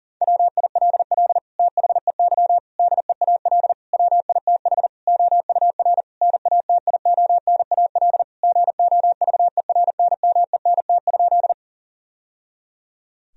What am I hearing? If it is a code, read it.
Morse code, heard as a keyed tone, about 40 wpm, 700 Hz. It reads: WILL THEY DEAL WITH OUR NATIONAL GOVERNMENT?
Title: Sentences from Top 1000 Words - Review Part 4 - 40wpm